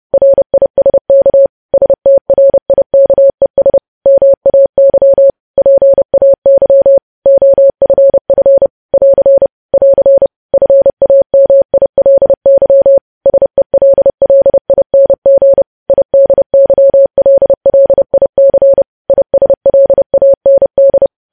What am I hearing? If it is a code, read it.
Morse code, heard as a keyed tone, about 30 wpm, 550 Hz. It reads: RISK STRIKES MAY PAY OFF <AR> <AR> FAMILY SELLING IDYLLIC ISLAND
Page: news headlines in morse code at 30 wpm